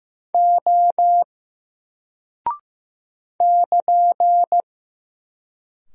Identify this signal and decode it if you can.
Morse code, heard as a keyed tone, about 15 wpm, 700 Hz. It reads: O <KN>
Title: Single Character (Letter:Number:Punctuation:Prosign) Alt 15wpm